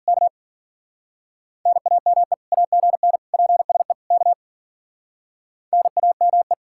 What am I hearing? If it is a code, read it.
Morse code, heard as a keyed tone, about 53 wpm, 700 Hz. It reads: K NAME AGN PSE K NAME